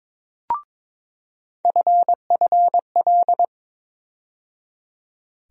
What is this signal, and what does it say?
Morse code, heard as a keyed tone, about 22 wpm, 700 Hz. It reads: FFL
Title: USA Call Sign Postfixes - 3 Letters 22wpm